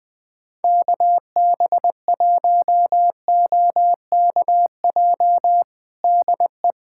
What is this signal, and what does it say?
Morse code, heard as a keyed tone, about 20 wpm, 700 Hz. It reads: KB1OKJ DE
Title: USA Callsign DE Callsign 20wpm